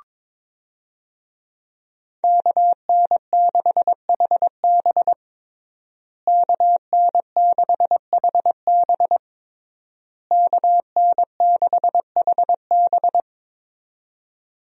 Morse code, heard as a keyed tone, about 22 wpm, 700 Hz. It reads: KN6HB KN6HB KN6HB
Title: USA Call Signs - 5 Characters - Repeated 3x - 22wpm